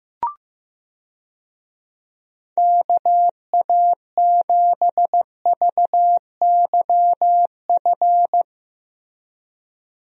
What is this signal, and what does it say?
Morse code, heard as a keyed tone, about 15 wpm, 700 Hz. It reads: KA7VYF
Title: USA Call Signs - 6 Characters - Repeated 3x - 15wpm